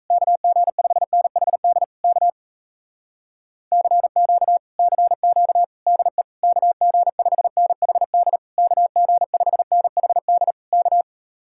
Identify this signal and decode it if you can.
Morse code, heard as a keyed tone, about 37 wpm, 700 Hz. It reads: KG5NHD K CQ CQ DE KG5NHD KG5NHD K